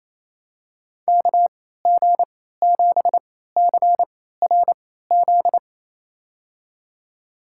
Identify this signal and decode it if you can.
Morse code, heard as a keyed tone, about 28 wpm, spaced as for 16 wpm, 700 Hz. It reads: KG7CRZ
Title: USA Call Signs - 3x Character Spacing - 28wpm